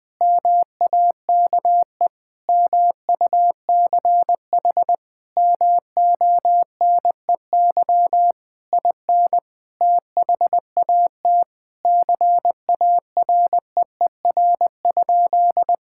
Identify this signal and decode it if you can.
Morse code, heard as a keyed tone, about 20 wpm, 700 Hz. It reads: MAKE MUCH MONEY IN THAT CAREER?
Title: Sentences from Top 1000 Words - Review Part 6 - 20wpm